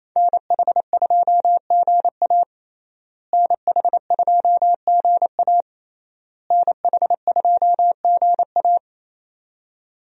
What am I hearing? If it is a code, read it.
Morse code, heard as a keyed tone, about 28 wpm, 700 Hz. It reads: NH2GA NH2GA NH2GA
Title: USA Call Signs - 5 Characters - Repeated 3x - 28wpm